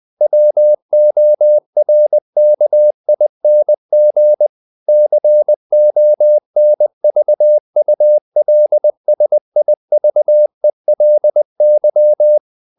Morse code, heard as a keyed tone, about 20 wpm, 600 Hz. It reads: WORKING CONVULSIVELY
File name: the old man and the sea part 2 20wpm